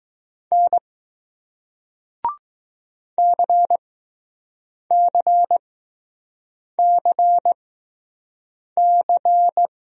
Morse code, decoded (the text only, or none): N CCCC